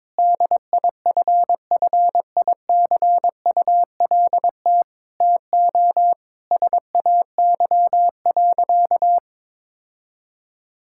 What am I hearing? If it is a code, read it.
Morse code, heard as a keyed tone, about 22 wpm, 700 Hz. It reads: DIFFICULT TO SAY.
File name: practice / 22wpm / Sentences with 4-Word Phrases - 22wpm